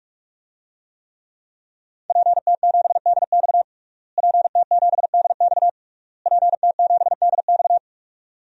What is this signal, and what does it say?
Morse code, heard as a keyed tone, about 41 wpm, 700 Hz. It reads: PT7DX PT7DX PT7DX
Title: DXCC-100 Call Signs - 5 Characters - Repeated 3x - 45wpm